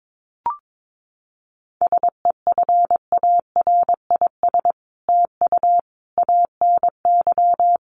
Morse code, heard as a keyed tone, about 22 wpm, 700 Hz. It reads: SEFARIS TU ANY
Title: A Sound Of Thunder 22wpm